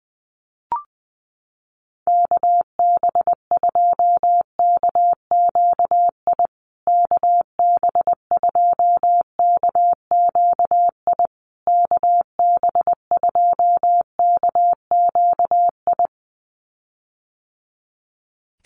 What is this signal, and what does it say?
Morse code, heard as a keyed tone, about 20 wpm, 700 Hz. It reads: KB2KQI KB2KQI KB2KQI
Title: USA Call Signs Repeated 3x 20wpm